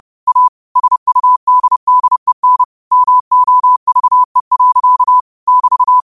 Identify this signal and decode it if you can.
Morse code, heard as a keyed tone, about 30 wpm, 1000 Hz. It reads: A SUDDEN MOVE. <BT>